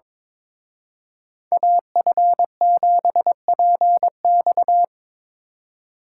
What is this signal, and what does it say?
Morse code, heard as a keyed tone, about 26 wpm, 700 Hz. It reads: AF7PX
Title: Speed Racing - USA 2x2 Call Signs 22wpm